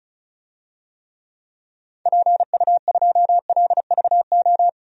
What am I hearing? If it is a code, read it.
Morse code, heard as a keyed tone, about 36 wpm, 700 Hz. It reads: PU2LVO
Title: DXCC-100 Call Signs - 6 Characters - Repeated 3x - 35wpm